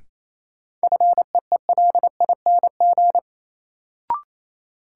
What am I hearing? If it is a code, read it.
Morse code, heard as a keyed tone, about 28 wpm, 700 Hz. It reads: FEELING
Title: Sentences from Top 1200 Words - New Word Drill Alt - 28wpm